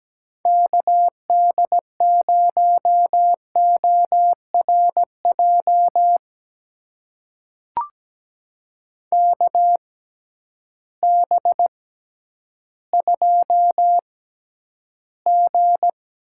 Morse code, heard as a keyed tone, about 17 wpm, spaced as for 10 wpm, 700 Hz. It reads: KD0ORJ KB2G
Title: USA Call Signs - 6 Characters - 6x Character Spacing - 17wpm